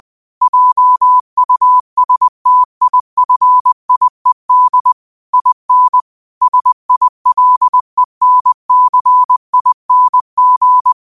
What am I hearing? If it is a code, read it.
Morse code, heard as a keyed tone, about 20 wpm, 1000 Hz. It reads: JUSTIFIED IN SILENCING